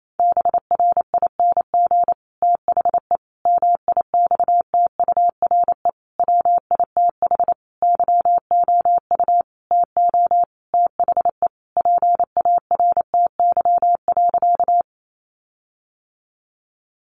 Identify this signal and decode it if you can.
Morse code, heard as a keyed tone, about 28 wpm, 700 Hz. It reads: BRING THE MIXTURE WITH YOU TO THE PARTY.
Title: Sentences from Top 1500 Words 28wpm